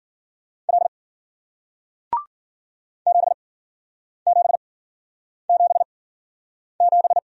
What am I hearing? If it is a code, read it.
Morse code, heard as a keyed tone, about 55 wpm, spaced as for 9 wpm, 700 Hz. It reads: L 7777